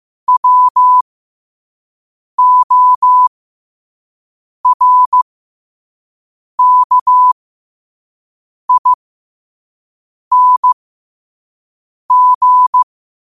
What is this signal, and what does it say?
Morse code, heard as a keyed tone, about 15 wpm, spaced as for 5 wpm, 1000 Hz. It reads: WORKING
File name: qotdfiles / 5wpm